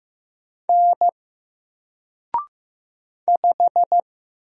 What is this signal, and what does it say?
Morse code, heard as a keyed tone, about 15 wpm, 700 Hz. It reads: N 5
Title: Single Letter-Number 15wpm